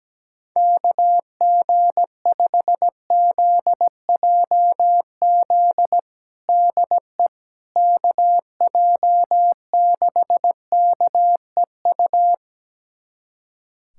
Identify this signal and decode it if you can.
Morse code, heard as a keyed tone, about 17 wpm, 700 Hz. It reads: KG5ZJZ DE KJ6KEU
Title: USA Callsign DE Callsign - 6 Characters - 17wpm